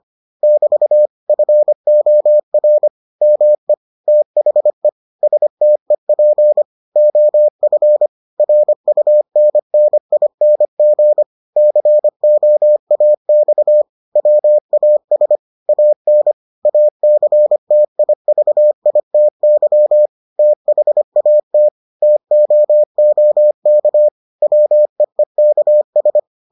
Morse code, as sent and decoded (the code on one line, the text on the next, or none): -...- / ..-. --- .-. / -- . / - .... / ... - . .--. / --- ..-. / .-. ..- -. -. .. -. --. / -.-. --- .- -..- / .-- .- ... / .- -. / .- -.-. - .. ...- .. - -.-- / - .... .- - / - --- --- -.- / .-- . -.- ...
<BT> FOR ME THE STEP OF RUNNING COAX WAS AN ACTIVITY THAT TOOK WEEKS